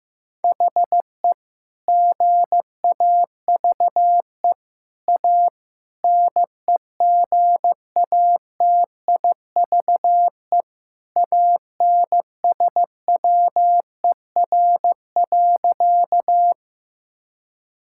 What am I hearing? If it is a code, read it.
Morse code, heard as a keyed tone, about 15 wpm, 700 Hz. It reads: HE GAVE A NEGATIVE ANSWER.
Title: Sentences from Top 1000 Words - Review Part 6 - 15wpm